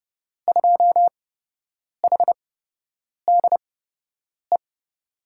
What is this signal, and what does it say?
Morse code, heard as a keyed tone, about 30 wpm, spaced as for 8 wpm, 700 Hz. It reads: 2HDE
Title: USA Call Signs - 6 Characters - 8x Character Spacing - 30wpm